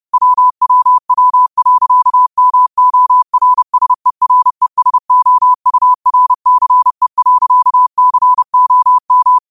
Morse code, heard as a keyed tone, about 30 wpm, 1000 Hz. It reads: WWW.MORSERESOURCE.COM
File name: qotdfiles / 30wpm